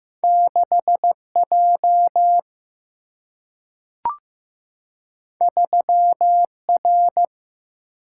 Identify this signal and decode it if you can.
Morse code, heard as a keyed tone, about 15 wpm, 700 Hz. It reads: 6 J 3 R
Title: Single-Letter Single-Number 15wpm